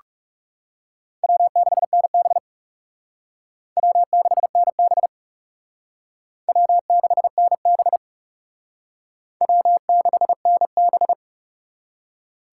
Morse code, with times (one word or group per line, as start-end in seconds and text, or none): 1.2-2.4 W6NB
3.8-5.1 W6NB
6.5-8.0 W6NB
9.4-11.1 W6NB